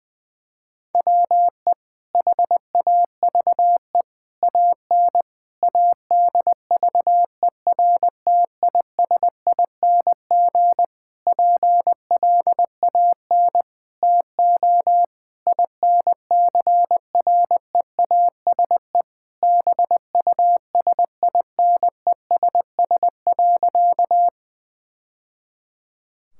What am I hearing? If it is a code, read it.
Morse code, heard as a keyed tone, about 20 wpm, 700 Hz. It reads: WE HAVE AN ADVERTISING PLAN TO INCREASE BUSINESS.
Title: Sentences from Top 1000 Words - Review Part 6 - 20wpm